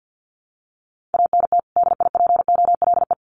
Morse code, heard as a keyed tone, about 50 wpm, 700 Hz. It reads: ANT DIPOLE
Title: Common QSO Phrases 50wpm